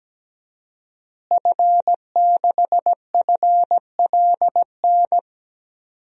Morse code, heard as a keyed tone, about 17 wpm, 700 Hz. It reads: F6FLN
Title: DXCC-100 Call Signs - 5 Characters - Repeated 3x - 17wpm